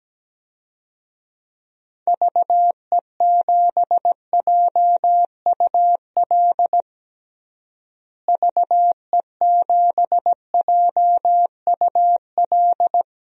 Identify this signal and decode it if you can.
Morse code, heard as a keyed tone, about 17 wpm, 700 Hz. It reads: VE7JUL VE7JUL
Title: DXCC-100 Call Signs - 6 Characters - Repeated 3x - 17wpm